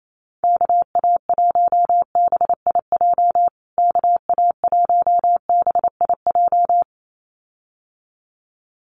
Morse code, heard as a keyed tone, about 28 wpm, 700 Hz. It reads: KA1BIJ KA1BIJ
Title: USA Call Signs Repeated 2x 28wpm